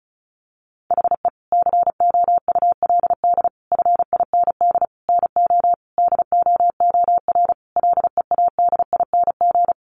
Morse code, heard as a keyed tone, about 34 wpm, 700 Hz. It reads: HE COULD FIND NO DOOR LEADING
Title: Sentences from Top 1100 Words 35wpm